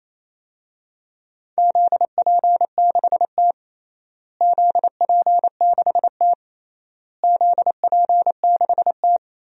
Morse code, heard as a keyed tone, about 28 wpm, 700 Hz. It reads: ZP6T ZP6T ZP6T
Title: DXCC-100 Call Signs - 4 Characters - Repeated 3x - 28wpm